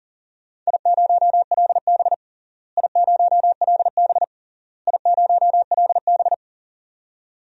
Morse code, heard as a keyed tone, about 37 wpm, 700 Hz. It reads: I0LB I0LB I0LB
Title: DXCC-100 Call Signs - 4 Characters - Repeated 3x - 40wpm